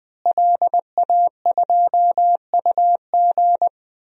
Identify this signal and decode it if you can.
Morse code, heard as a keyed tone, about 21 wpm, 700 Hz. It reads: LA2UG
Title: DXCC-100 Call Signs - 5 Characters - Repeated 2x - 20wpm